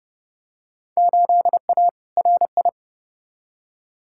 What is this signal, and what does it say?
Morse code, heard as a keyed tone, about 30 wpm, 700 Hz. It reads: 8A RI
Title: Field Day Exchanges Encore - 30wpm